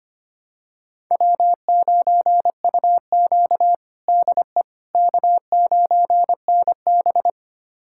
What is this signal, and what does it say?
Morse code, heard as a keyed tone, about 25 wpm, 700 Hz. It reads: W9UQ DE K9NB
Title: USA Callsign DE Callsign - 4 Characters - 25wpm